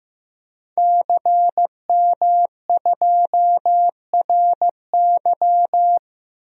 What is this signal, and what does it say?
Morse code, heard as a keyed tone, about 15 wpm, 700 Hz. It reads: CM2RY
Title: DXCC-100 Call Signs - 5 Characters - 15wpm